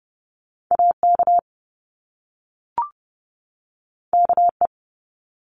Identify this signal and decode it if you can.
Morse code, heard as a keyed tone, about 30 wpm, 700 Hz. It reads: AK KE
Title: Top Two Letter Combinations in Words 30wpm